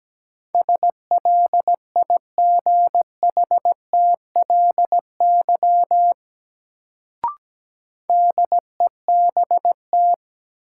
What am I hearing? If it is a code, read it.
Morse code, heard as a keyed tone, about 17 wpm, 700 Hz. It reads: SLIGHTLY DEBT
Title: Sentences from Top 1300 Words - New Word Drill Alt - 17wpm